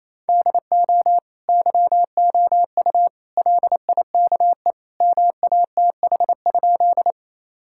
Morse code, heard as a keyed tone, about 28 wpm, 700 Hz. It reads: DO YOU LIKE MATH?
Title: Sentences from Top 1000 Words - Review Part 6 - 28wpm